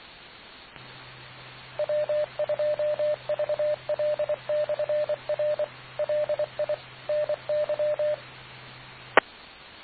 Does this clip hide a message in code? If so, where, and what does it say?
Yes: Morse code from 1.8 to 9.2 s, heard as a keyed tone, about 24 wpm, 590 Hz, with radio noise. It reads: W2VL/R LI NY E